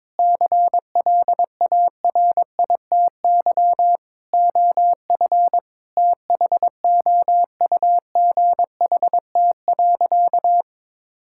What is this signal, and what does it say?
Morse code, heard as a keyed tone, about 22 wpm, 700 Hz. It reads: CLARITY OF THOUGHT.
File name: Difficult Sentences with Hints - 22wpm